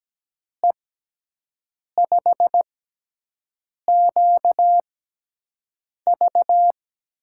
Morse code, heard as a keyed tone, about 17 wpm, spaced as for 6 wpm, 700 Hz. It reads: E5QV
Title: USA Call Signs - 6 Characters - 6x Character Spacing - 17wpm